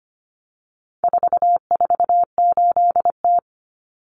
Morse code, heard as a keyed tone, about 25 wpm, 700 Hz. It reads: 448T
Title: Common Cut Numbers - 4-Digit 25wpm